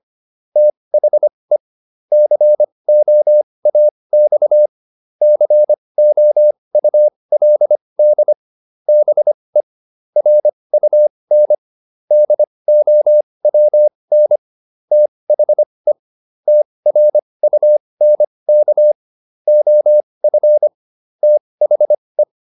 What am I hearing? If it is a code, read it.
Morse code, heard as a keyed tone, about 25 wpm, spaced as for 20 wpm, 600 Hz. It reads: THE COAX COULD BE RUN DOWN THE TRUNK OF THE